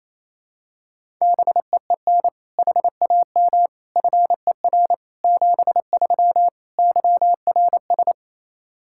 Morse code, heard as a keyed tone, about 28 wpm, 700 Hz. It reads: BEEN HAM FER 73 YRS
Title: Practice QSO - Exchange 3 - More Info - 28wpm